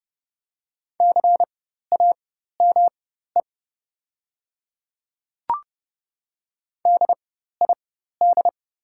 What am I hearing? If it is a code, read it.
Morse code, heard as a keyed tone, about 30 wpm, spaced as for 14 wpm, 700 Hz. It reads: CAME DID
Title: 4x Character Spacing - Top 200 Words - 30wpm